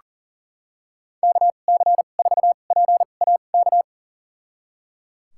Morse code, heard as a keyed tone, about 37 wpm, spaced as for 28 wpm, 700 Hz. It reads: KC4PAK
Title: USA Call Signs - 6 Characters - 2x Character Spacing - 40wpm